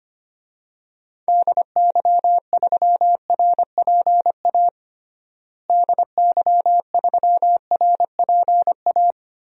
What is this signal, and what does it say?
Morse code, heard as a keyed tone, about 25 wpm, 700 Hz. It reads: DY3RPA DY3RPA
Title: DXCC-100 Call Signs - 6 Characters - Repeated 2x - 25wpm